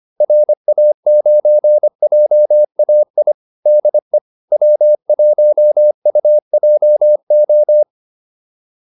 Morse code, heard as a keyed tone, about 25 wpm, 600 Hz. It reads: RA9JAI DE W1UJO
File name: QSO401a 25